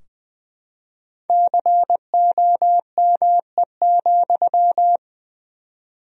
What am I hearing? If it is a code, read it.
Morse code, heard as a keyed tone, about 20 wpm, 700 Hz. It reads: COME,
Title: 4x Word-Spacing - Sentences from Top 300 Words - 20wpm